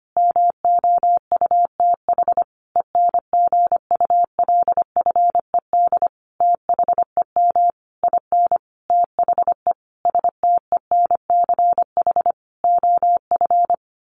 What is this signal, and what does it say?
Morse code, heard as a keyed tone, about 25 wpm, 700 Hz. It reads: MOUTH ENGULFED THEM IN THE STENCH OF
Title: A Sound Of Thunder 25wpm